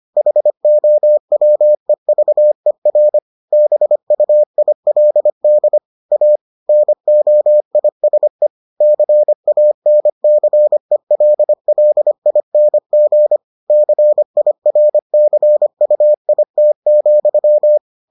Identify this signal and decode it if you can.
Morse code, heard as a keyed tone, about 25 wpm, 600 Hz. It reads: HOWEVER BUILD A NOISE CANCELLING CIRCUIT,